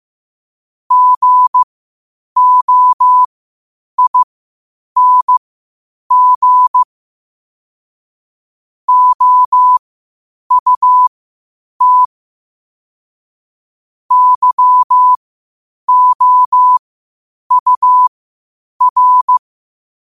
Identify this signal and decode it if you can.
Morse code, heard as a keyed tone, about 15 wpm, spaced as for 8 wpm, 1000 Hz. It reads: GOING OUT YOUR
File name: qotdfiles / 8wpm